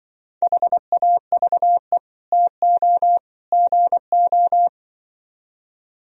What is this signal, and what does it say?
Morse code, heard as a keyed tone, about 24 wpm, 700 Hz. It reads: HAVE TO GO
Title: Speed-Racing - Sets of 3 Words 20wpm